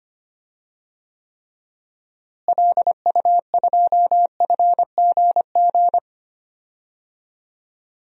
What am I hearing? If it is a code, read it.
Morse code, heard as a keyed tone, about 25 wpm, 700 Hz. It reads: LU2FGG
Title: DXCC-100 Call Signs - 6 Characters - Repeated 2x - 25wpm